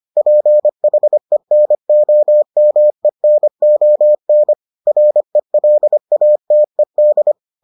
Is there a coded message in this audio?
Yes: Morse code from 0.2 to 7.3 s, heard as a keyed tone, about 25 wpm, 600 Hz. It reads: PHENOMENON RELATED